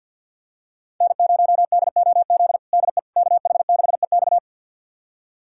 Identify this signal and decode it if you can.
Morse code, heard as a keyed tone, about 47 wpm, 700 Hz. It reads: N0DOZ DE KH6EX
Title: USA Callsign DE Callsign - 5 Characters - 50wpm